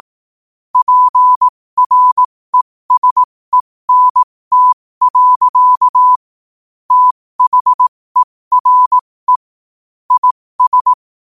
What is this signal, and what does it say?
Morse code, heard as a keyed tone, about 18 wpm, spaced as for 15 wpm, 1000 Hz. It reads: PRESENT. THERE IS
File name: qotdfiles / 15wpm